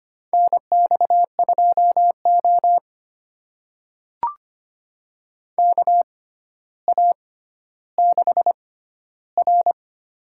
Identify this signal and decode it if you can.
Morse code, heard as a keyed tone, about 25 wpm, spaced as for 13 wpm, 700 Hz. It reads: NX2O KA6R
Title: USA Call Signs - 4 Characters - 6x Character Spacing - 25wpm